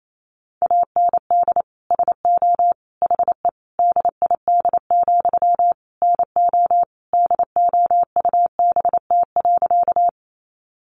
Morse code, heard as a keyed tone, about 28 wpm, 700 Hz. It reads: AND SO HE DID, NO DOUBT.
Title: Sentences from Top 1500 Words 28wpm